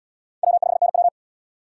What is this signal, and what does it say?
Morse code, heard as a keyed tone, about 70 wpm, 700 Hz. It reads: W5NW